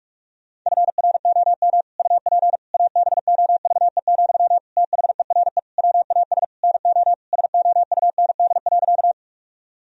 Morse code, heard as a keyed tone, about 46 wpm, 700 Hz. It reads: FROM UP ABOVE, THERE WAS NO SOUND.
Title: Sentences from Top 1000 Words - Review Part 2 - 45wpm